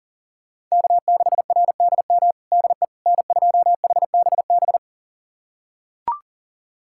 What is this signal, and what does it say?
Morse code, heard as a keyed tone, about 40 wpm, 700 Hz. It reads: K6RDM DE N2HBB
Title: USA Callsign DE Callsign - 5 Characters - 40wpm